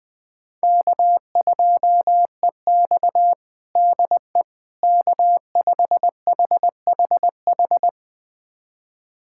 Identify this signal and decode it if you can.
Morse code, heard as a keyed tone, about 20 wpm, 700 Hz. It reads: K2EX DE K5HHH
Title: USA Callsign DE Callsign 20wpm